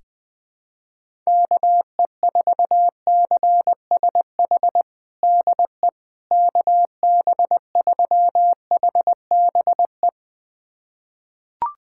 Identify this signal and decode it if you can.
Morse code, heard as a keyed tone, about 20 wpm, 700 Hz. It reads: KE4CSH DE KB3HBE
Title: USA Callsign DE Callsign 20wpm